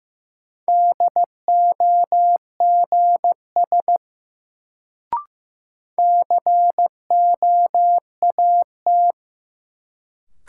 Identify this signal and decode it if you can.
Morse code, heard as a keyed tone, about 15 wpm, 700 Hz. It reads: DOGS COAT
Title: Sentences from Top 1600 Words - New Word Drill Alt - 15wpm